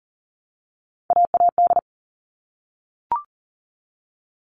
Morse code, heard as a keyed tone, about 40 wpm, 700 Hz. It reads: AAD
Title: All Cut Numbers - 3-Digit 40wpm